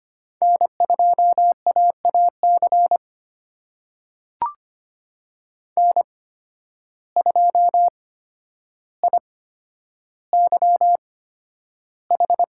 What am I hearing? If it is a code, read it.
Morse code, heard as a keyed tone, about 25 wpm, spaced as for 11 wpm, 700 Hz. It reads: N2AAC N2IYH
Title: USA Call Signs - 5 Characters - 8x Character Spacing - 25wpm